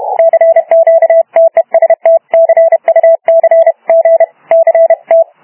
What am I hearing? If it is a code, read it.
Morse code, heard as a keyed tone, about 31 wpm, 660 Hz. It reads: CQ TEST CUC G CT
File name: Archiv / CUC318 / CUC318 záznam OK1DCS